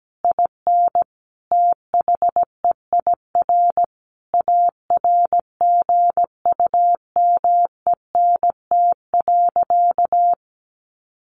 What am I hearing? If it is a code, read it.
Morse code, heard as a keyed tone, about 17 wpm, 700 Hz. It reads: IN THEIR ARGUMENT.
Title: Sentences from Top 1500 Words 17wpm